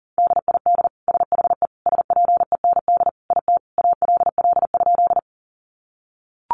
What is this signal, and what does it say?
Morse code, heard as a keyed tone, about 40 wpm, 690 Hz. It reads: DID SHE SPEND IT ALL?
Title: Sentences from Top 800 Words 40wpm